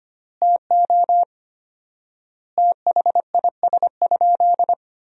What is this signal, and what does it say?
Morse code, heard as a keyed tone, about 25 wpm, 700 Hz. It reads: TO THIS?
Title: 4x Word-Spacing - Sentences from Top 100 Words - 25wpm